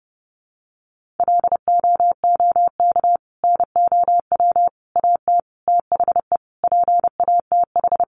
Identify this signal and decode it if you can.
Morse code, heard as a keyed tone, about 28 wpm, 700 Hz. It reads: LOOK NOW AT THE PATH
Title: Sentences from Top 1400 Words 30wpm